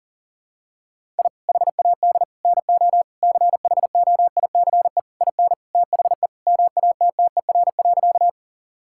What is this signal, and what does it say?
Morse code, heard as a keyed tone, about 40 wpm, 700 Hz. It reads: I HAD NO CHOICE IN THE MATTER.
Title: Sentences from Top 1000 Words - Review Part 6 - 40wpm